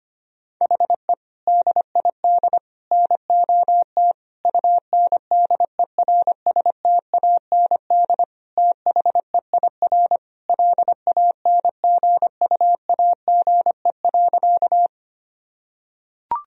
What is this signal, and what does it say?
Morse code, heard as a keyed tone, about 25 wpm, 700 Hz. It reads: HE DID NOT UNDERSTAND THEIR LANGUAGE.
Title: Sentences from Top 1000 Words - Review Part 3 - 25wpm